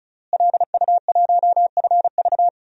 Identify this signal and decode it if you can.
Morse code, heard as a keyed tone, about 36 wpm, 700 Hz. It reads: LU1FV
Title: DXCC-100 Call Signs - 5 Characters - Repeated 3x - 35wpm